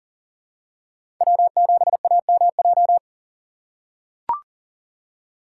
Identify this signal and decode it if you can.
Morse code, heard as a keyed tone, about 39 wpm, 700 Hz. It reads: W7AMJ
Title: USA Call Signs - 5 Characters - 6x Character Spacing - 40wpm